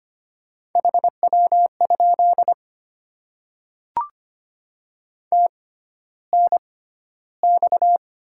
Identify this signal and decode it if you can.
Morse code, heard as a keyed tone, about 25 wpm, spaced as for 13 wpm, 700 Hz. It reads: HW? TNX
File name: Top 20 QSO Elements - 6x Character Spacing - 25wpm